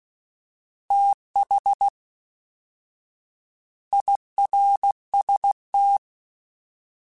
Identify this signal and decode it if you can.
Morse code, heard as a keyed tone, about 16 wpm, 800 Hz. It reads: TH IRST